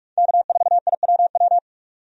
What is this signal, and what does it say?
Morse code, heard as a keyed tone, about 46 wpm, 700 Hz. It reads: K4IWW
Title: USA Call Signs - 5 Characters - 45wpm